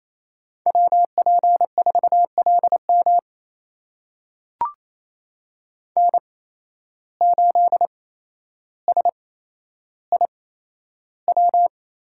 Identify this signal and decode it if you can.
Morse code, heard as a keyed tone, about 28 wpm, spaced as for 12 wpm, 700 Hz. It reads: WP4LM N8SIW
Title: USA Call Signs - 5 Characters - 8x Character Spacing - 28wpm